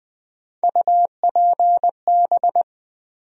Morse code, heard as a keyed tone, about 20 wpm, 700 Hz. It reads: UPB
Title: USA Call Sign Postfixes - 3 Letters 20wpm